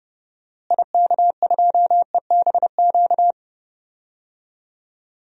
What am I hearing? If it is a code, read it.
Morse code, heard as a keyed tone, about 30 wpm, 700 Hz. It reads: IK2EBQ
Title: DXCC-100 Call Signs - 6 Characters - Repeated 3x - 30wpm